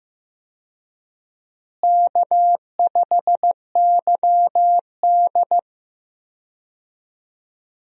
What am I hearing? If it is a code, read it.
Morse code, heard as a keyed tone, about 15 wpm, 700 Hz. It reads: K5YD
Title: USA Call Signs - 4 Characters - Repeated 3x - 15wpm